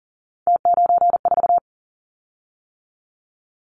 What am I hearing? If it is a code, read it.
Morse code, heard as a keyed tone, about 38 wpm, 700 Hz. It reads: T94
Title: Callsign Prefixes for Top 50 DXCC - 40wpm